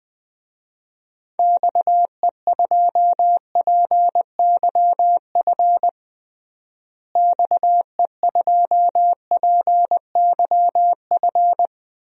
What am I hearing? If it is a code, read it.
Morse code, heard as a keyed tone, about 20 wpm, 700 Hz. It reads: XE2PYF XE2PYF
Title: DXCC-100 Call Signs - 6 Characters - Repeated 2x - 20wpm